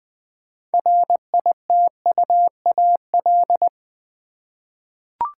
Morse code, heard as a keyed tone, about 20 wpm, 700 Hz. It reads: RITUAL
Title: 6 Letter Words - First 3 Letters Given - 20wpm